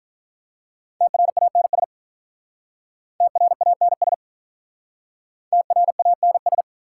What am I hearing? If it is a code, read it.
Morse code, heard as a keyed tone, about 48 wpm, 700 Hz. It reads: TRANS TRANS TRANS